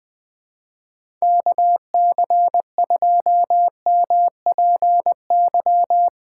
Morse code, heard as a keyed tone, about 20 wpm, 700 Hz. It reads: KC2MPY
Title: USA Call Signs - 6 Characters - 4x Character Spacing - 20wpm